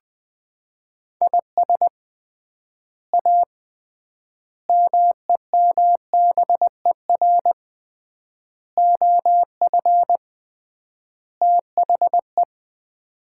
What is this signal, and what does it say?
Morse code, heard as a keyed tone, about 20 wpm, 700 Hz. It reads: IS A MEMBER OF THE
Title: 3x Word-Spacing - Sets of 5 Words - 20wpm